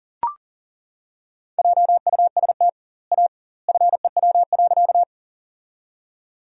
Morse code, heard as a keyed tone, about 40 wpm, 700 Hz. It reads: JUST A FEW.
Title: 2x Word-Spacing - Sentences from Top 300 Words - 40wpm